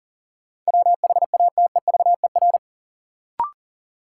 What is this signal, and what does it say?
Morse code, heard as a keyed tone, about 40 wpm, 700 Hz. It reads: WHATEVER
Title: Sentences from Top 1100 Words - New Word Drill - 40wpm